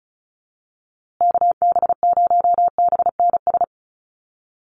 Morse code, heard as a keyed tone, about 34 wpm, 700 Hz. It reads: KB0BNS
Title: USA Call Signs Repeated 2x 35wpm